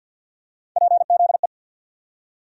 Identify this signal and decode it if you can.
Morse code, heard as a keyed tone, about 53 wpm, 700 Hz. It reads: PZE
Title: USA Call Sign Postfixes - 3 Letters 50wpm